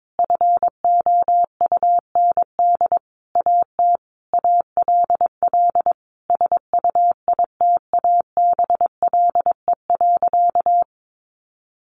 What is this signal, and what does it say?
Morse code, heard as a keyed tone, about 22 wpm, 700 Hz. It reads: FOUND AT ALL SUITABLE.
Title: Sentences from Top 1400 Words 22wpm